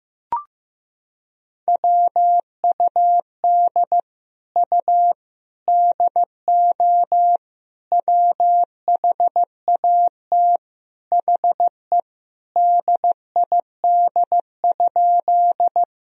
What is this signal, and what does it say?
Morse code, heard as a keyed tone, about 16 wpm, 700 Hz. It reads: WUD U DO WHAT HE DID?
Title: Sentences from Top 100 Words and Abbreviations 15wpm